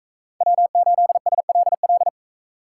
Morse code, heard as a keyed tone, about 39 wpm, 700 Hz. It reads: W8SLL